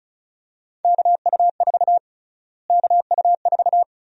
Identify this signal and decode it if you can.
Morse code, heard as a keyed tone, about 36 wpm, 700 Hz. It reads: KU4 KU4
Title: USA Call Sign Prefixes - Repeated 2x - 35wpm